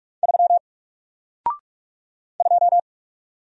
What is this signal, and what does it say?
Morse code, heard as a keyed tone, about 44 wpm, 700 Hz. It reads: 3 2
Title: Single Character (Letter:Number:Punctuation:Prosign) 45wpm